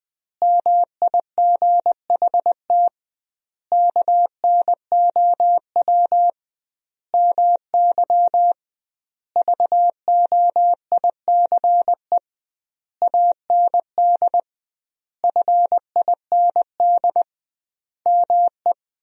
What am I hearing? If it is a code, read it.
Morse code, heard as a keyed tone, about 20 wpm, 700 Hz. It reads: MIGHT KNOW MY VOICE AND FIND ME
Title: 2x Word-Spacing - Sentences from Top 500 Words - 20wpm